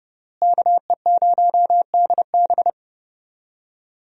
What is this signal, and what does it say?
Morse code, heard as a keyed tone, about 30 wpm, 700 Hz. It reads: KE0DB
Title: USA Call Signs - 5 Characters - Repeated 3x - 30wpm